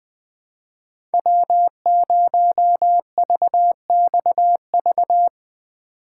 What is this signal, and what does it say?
Morse code, heard as a keyed tone, about 20 wpm, 700 Hz. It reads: W0VXV
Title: USA Call Signs - 5 Characters - 2x Character Spacing - 20wpm